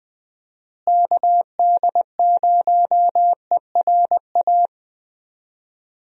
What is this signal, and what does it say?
Morse code, heard as a keyed tone, about 20 wpm, 700 Hz. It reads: KD0ERA
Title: USA Call Signs - 6 Characters - Repeated 2x - 20wpm